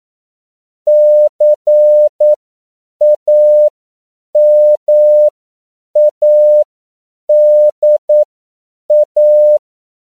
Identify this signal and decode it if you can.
Morse code, heard as a keyed tone, about 9 wpm, spaced as for 7 wpm, 600 Hz. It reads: CAMADA
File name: CW / 10PPM